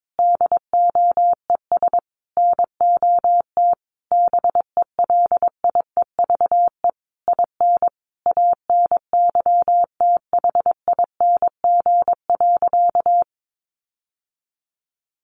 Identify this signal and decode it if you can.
Morse code, heard as a keyed tone, about 22 wpm, 700 Hz. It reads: DOES NOT BELIEVE IN ANYTHING.
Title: Sentences from Top 1500 Words 22wpm